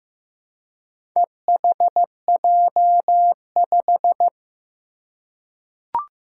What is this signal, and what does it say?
Morse code, heard as a keyed tone, about 15 wpm, 700 Hz. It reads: EHJ5
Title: Four Challenging Letter-Number Combinations 15wpm